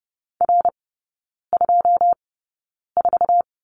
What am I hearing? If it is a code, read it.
Morse code, heard as a keyed tone, about 30 wpm, spaced as for 9 wpm, 700 Hz. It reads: R24
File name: 081 Lesson 14e - Number 2 - Three Characters 30wpm